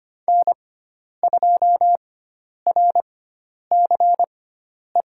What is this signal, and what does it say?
Morse code, heard as a keyed tone, about 25 wpm, spaced as for 10 wpm, 700 Hz. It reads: N2RCE
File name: Lesson 16g - Letter c - Call Signs 25wpm